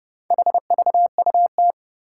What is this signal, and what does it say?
Morse code, heard as a keyed tone, about 30 wpm, 700 Hz. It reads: HVUT
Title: Four Challenging Letter-Number Combinations 30wpm